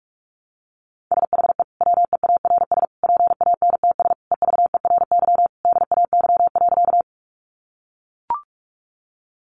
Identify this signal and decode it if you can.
Morse code, heard as a keyed tone, about 44 wpm, 700 Hz. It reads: SHE WEARS PANTS EVERY DAY.
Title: Additional Easy Phrases 45wpm